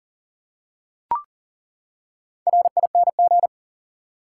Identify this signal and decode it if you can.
Morse code, heard as a keyed tone, about 40 wpm, 700 Hz. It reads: RING
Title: Four Letter Words - 40wpm